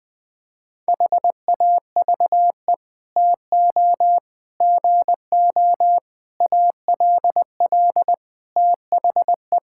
Morse code, heard as a keyed tone, about 20 wpm, 700 Hz. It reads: HAVE TO GO ALL THE
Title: Sets of 8 Words - 20wpm